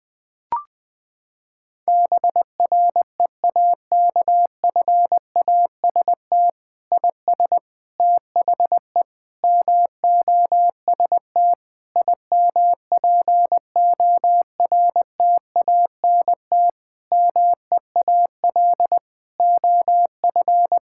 Morse code, heard as a keyed tone, about 21 wpm, 700 Hz. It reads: BREAKFAST IS THE MOST IMPORTANT MEAL OF
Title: Sets of 9 Words - 20wpm